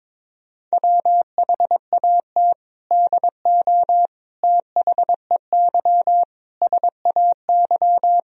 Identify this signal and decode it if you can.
Morse code, heard as a keyed tone, about 21 wpm, 700 Hz. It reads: WHAT DO THEY SAY
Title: Sentences from Top 1000 Words - Review Part 1 - 22wpm